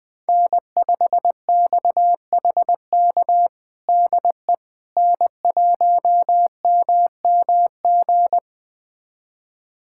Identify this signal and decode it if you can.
Morse code, heard as a keyed tone, about 20 wpm, 700 Hz. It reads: N5XHK DE N1MMG
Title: USA Callsign DE Callsign - 5 Characters - 20wpm